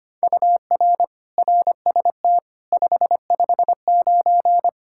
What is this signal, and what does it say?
Morse code, heard as a keyed tone, about 25 wpm, 700 Hz. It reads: UR RST 559
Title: Practice QSO - Exchange 1 - Beginning - 25wpm